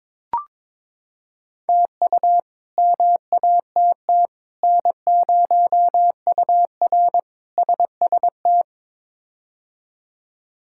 Morse code, heard as a keyed tone, about 22 wpm, 700 Hz. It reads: TU MATT N0UR SST
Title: K1USN SST QSO Practice - 22wpm